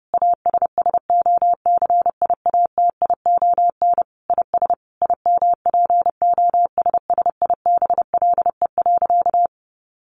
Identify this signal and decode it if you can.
Morse code, heard as a keyed tone, about 30 wpm, 700 Hz. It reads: ASSOCIATION IS IMPOSSIBLE.
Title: Sentences from Top 1100 Words 30wpm